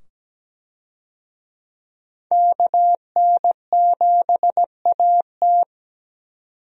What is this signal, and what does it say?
Morse code, heard as a keyed tone, about 17 wpm, 700 Hz. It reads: KN7AT
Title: USA Call Signs - 5 Characters - 17wpm